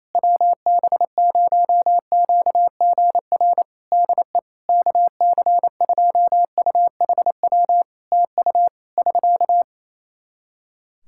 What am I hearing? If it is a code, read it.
Morse code, heard as a keyed tone, about 28 wpm, 700 Hz. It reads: WB0QGR DE KC2UHW TU <SK>
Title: Practice QSO - Exchange 4 - Ending - 28wpm